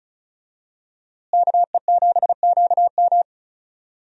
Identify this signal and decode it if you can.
Morse code, heard as a keyed tone, about 34 wpm, 700 Hz. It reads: KE7QM
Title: USA Call Signs - 5 Characters - Repeated 2x - 35wpm